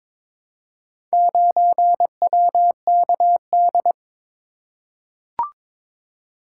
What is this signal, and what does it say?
Morse code, heard as a keyed tone, about 22 wpm, 700 Hz. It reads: 9WKD
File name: practice / 22wpm / Four Letter-Number Combinations - 22wpm